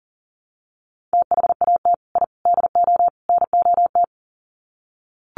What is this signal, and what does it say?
Morse code, heard as a keyed tone, about 40 wpm, 700 Hz. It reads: THAT I DO NOT
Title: Sets of 4 Words 40wpm